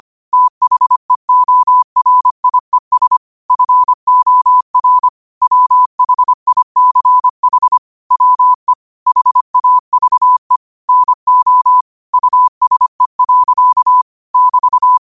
Morse code, heard as a keyed tone, about 25 wpm, 1000 Hz. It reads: THEORIES FOR WHICH WE HAVE NO USE. <BT>
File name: qotdfiles / 25wpm